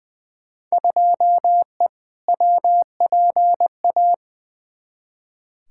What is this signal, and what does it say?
Morse code, heard as a keyed tone, about 20 wpm, 700 Hz. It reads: 2E WPA
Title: Field Day Exchanges - 20wpm